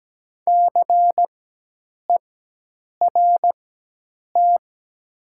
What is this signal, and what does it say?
Morse code, heard as a keyed tone, about 17 wpm, spaced as for 8 wpm, 700 Hz. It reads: CERT
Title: 4x Character Spacing - Top 500 Words - 17wpm